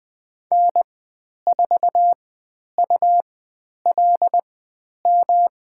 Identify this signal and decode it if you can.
Morse code, heard as a keyed tone, about 20 wpm, spaced as for 10 wpm, 700 Hz. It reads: N4ULM
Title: 109 Lesson 18g - Letter m - Call Signs 20wpm